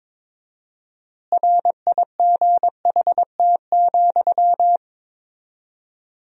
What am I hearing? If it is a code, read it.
Morse code, heard as a keyed tone, about 22 wpm, 700 Hz. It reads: RIGHT,
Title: 4x Word-Spacing - Sentences from Top 200 Words - 22wpm